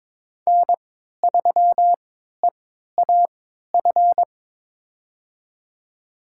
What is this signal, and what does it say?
Morse code, heard as a keyed tone, about 22 wpm, spaced as for 13 wpm, 700 Hz. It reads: N3EAF
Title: USA Call Signs - 5 Characters - 3x Character Spacing - 22wpm